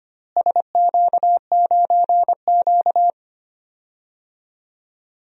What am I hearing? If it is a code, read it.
Morse code, heard as a keyed tone, about 25 wpm, 700 Hz. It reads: SQ9Q
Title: DXCC-100 Call Signs - 4 Characters - Repeated 3x - 25wpm